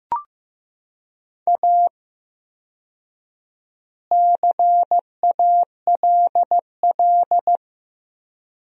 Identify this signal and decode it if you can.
Morse code, heard as a keyed tone, about 15 wpm, 700 Hz. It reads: A CALL
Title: 4x Word-Spacing - Sentences from Top 300 Words - 15wpm